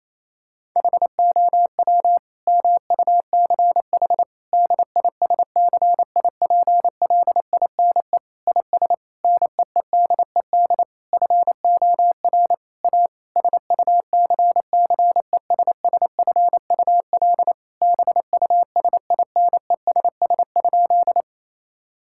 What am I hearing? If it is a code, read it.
Morse code, heard as a keyed tone, about 28 wpm, 700 Hz. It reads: HOW MUCH DISCIPLINE IS NEEDED FOR A SUCCESSFUL BUSINESS?
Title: Sentences from Top 1000 Words - Review Part 6 - 28wpm